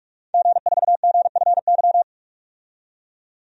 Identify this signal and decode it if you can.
Morse code, heard as a keyed tone, about 47 wpm, 700 Hz. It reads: G4GFY
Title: DXCC-100 Call Signs - 5 Characters - 45wpm